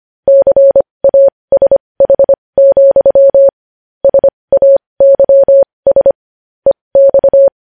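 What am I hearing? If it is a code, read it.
Morse code, heard as a keyed tone, about 25 wpm, spaced as for 20 wpm, 550 Hz. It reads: CASH, SAYS EX
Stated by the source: Morse practice of news headlines at 20 wpm.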